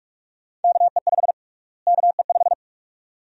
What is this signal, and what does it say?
Morse code, heard as a keyed tone, about 42 wpm, 700 Hz. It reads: KE5 KE5
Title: USA Call Sign Prefixes - Repeated 2x - 45wpm